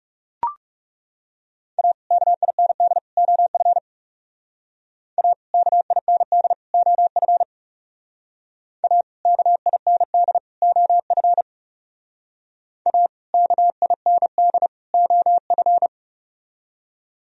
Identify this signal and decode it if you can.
Morse code, heard as a keyed tone, about 35 wpm, 700 Hz. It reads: A KIND OF A KIND OF A KIND OF A KIND OF
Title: Speed-Racing - Sets of 3 Words 30wpm